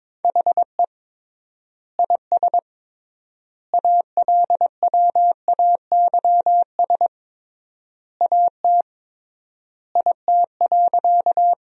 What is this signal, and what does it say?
Morse code, heard as a keyed tone, about 22 wpm, 700 Hz. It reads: HE IS ALWAYS AT IT.
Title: 3x Word-Spacing - Sentences from Top 300 Words - 22wpm